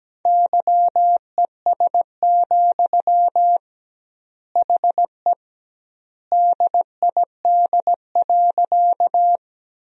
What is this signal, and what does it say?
Morse code, heard as a keyed tone, about 17 wpm, 700 Hz. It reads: YES, HE DID.
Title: 2x Word-Spacing - Sentences from Top 500 Words - 17wpm